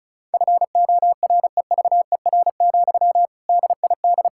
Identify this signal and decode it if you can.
Morse code, heard as a keyed tone, about 36 wpm, 700 Hz. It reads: FOREVER, DID
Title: Sentences from Top 1600 Words 35wpm